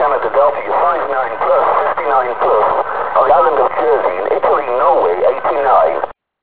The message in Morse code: none